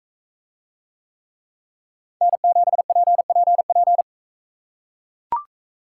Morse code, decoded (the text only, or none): N7PPP